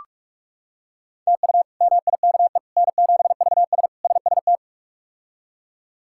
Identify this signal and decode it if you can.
Morse code, heard as a keyed tone, about 46 wpm, 700 Hz. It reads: TU MIKE N7VS SST
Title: K1USN SST QSO Practice - 45wpm